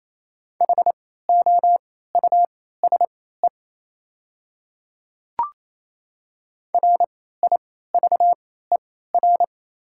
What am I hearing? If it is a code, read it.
Morse code, heard as a keyed tone, about 28 wpm, spaced as for 16 wpm, 700 Hz. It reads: HOUSE RIVER
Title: 3x Character Spacing - Top 300 Words - 28wpm